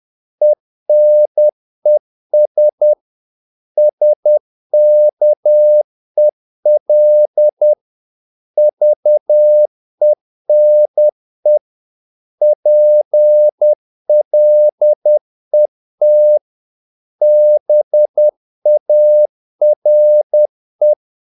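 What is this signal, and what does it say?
Morse code, heard as a keyed tone, about 10 wpm, 600 Hz. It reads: ENES SKEL VENE PLET BARE
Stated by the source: Morse practice of short Danish words at 10 wpm, from CW med Gnister.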